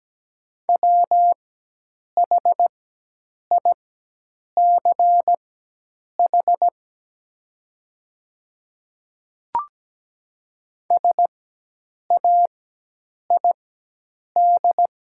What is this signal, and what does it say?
Morse code, heard as a keyed tone, about 17 wpm, spaced as for 8 wpm, 700 Hz. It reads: WHICH SAID
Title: 4x Character Spacing - Top 100 Words - 17wpm